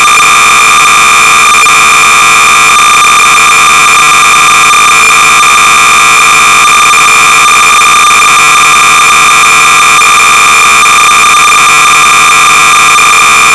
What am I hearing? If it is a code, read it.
Cassette data (Kansas City type).
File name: sorcerer / 300baud / wordsquares-(date and author unknow)-BASIC-Sorcerer-300baud